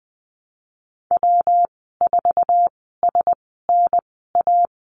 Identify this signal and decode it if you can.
Morse code, heard as a keyed tone, about 20 wpm, spaced as for 15 wpm, 700 Hz. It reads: W4SNA
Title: 154 Lesson 25g - Letter p - Call Signs 20wpm